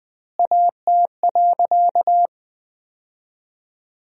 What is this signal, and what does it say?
Morse code, heard as a keyed tone, about 20 wpm, 700 Hz. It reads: AT.
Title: 4x Word-Spacing - Sentences from Top 200 Words - 20wpm